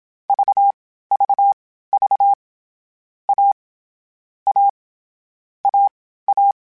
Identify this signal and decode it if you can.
Morse code, heard as a keyed tone, about 27 wpm, spaced as for 15 wpm, 800 Hz. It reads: VVV A A AA
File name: Difficult Combos / Difficult Combos-mp3s / AWJ ShortRun@25